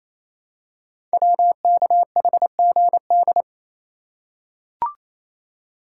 Morse code, heard as a keyed tone, about 28 wpm, 700 Hz. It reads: WKHGD
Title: Code Groups - 5 Letters - 28wpm